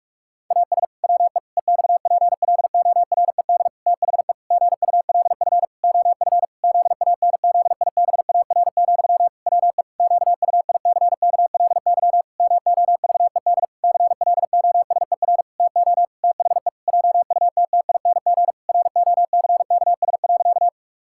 Morse code, heard as a keyed tone, about 46 wpm, 700 Hz. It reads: AS WE EXPLORED THE GULF OF ZANZIBAR, WE QUICKLY MOVED CLOSER TO THE JUTTING ROCKS.
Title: Pangram Challenge v2 - 45wpm